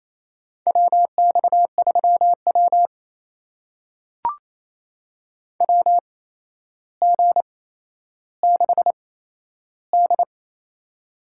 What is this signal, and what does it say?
Morse code, heard as a keyed tone, about 28 wpm, spaced as for 12 wpm, 700 Hz. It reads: WX3W WG6D
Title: USA Call Signs - 4 Characters - 8x Character Spacing - 28wpm